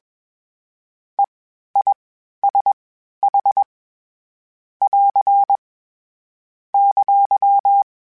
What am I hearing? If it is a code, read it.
Morse code, heard as a keyed tone, about 21 wpm, spaced as for 12 wpm, 800 Hz. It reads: EISH <AR> !
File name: Short Runs / Short Runs-mp3s / All DITs @20